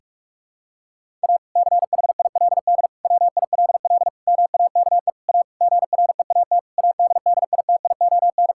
Morse code, heard as a keyed tone, about 41 wpm, 700 Hz. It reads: A CHILD WILL MAKE A GREAT ADDITION
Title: Sentences from Top 1000 Words - Review Part 6 - 45wpm